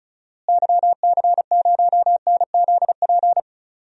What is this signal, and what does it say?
Morse code, heard as a keyed tone, about 34 wpm, 700 Hz. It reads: YC0NZP
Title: DXCC-100 Call Signs - 6 Characters - Repeated 2x - 35wpm